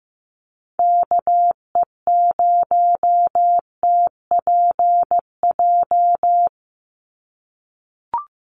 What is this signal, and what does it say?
Morse code, heard as a keyed tone, about 15 wpm, 700 Hz. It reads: KE0TPJ
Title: USA Call Signs 15wpm